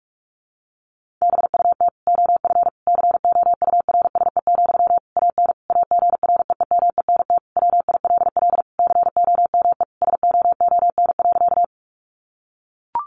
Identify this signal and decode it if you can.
Morse code, heard as a keyed tone, about 44 wpm, 700 Hz. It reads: BUT OF COURSE, AN AGREEMENT WILL COME SOON.
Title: Sentences from Top 1500 Words 45wpm